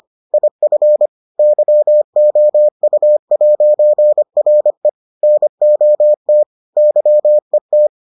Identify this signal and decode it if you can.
Morse code, heard as a keyed tone, about 25 wpm, 600 Hz. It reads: IF YOU'RE NOT YET